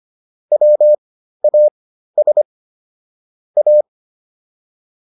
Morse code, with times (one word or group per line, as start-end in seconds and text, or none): 0.5-2.4 WAS
3.6-3.8 A